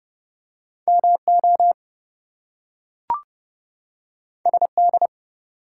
Morse code, heard as a keyed tone, about 30 wpm, 700 Hz. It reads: MO SD
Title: US States and Canadian Provinces 30wpm